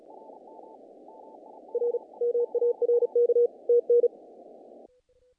Morse code, heard as a keyed tone, about 34 wpm, 480 Hz. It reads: R MARK TN
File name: contests / naqp-cw-k5tr-op-k5pi- -1800